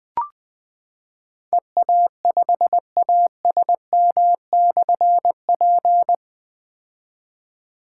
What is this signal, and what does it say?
Morse code, heard as a keyed tone, about 20 wpm, 700 Hz. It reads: EA5ASM/P
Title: Difficult Call Signs 20wpm